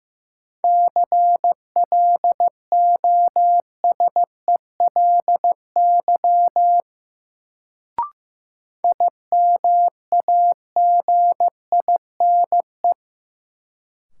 Morse code, heard as a keyed tone, about 16 wpm, 700 Hz. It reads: CLOSELY IMAGINE
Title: Sentences from Top 1400 Words - New Word Drill - 15wpm